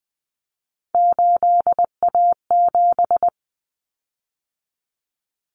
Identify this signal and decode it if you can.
Morse code, heard as a keyed tone, about 20 wpm, 700 Hz. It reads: 8A7
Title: Common Cut Numbers - 3-Digit 20wpm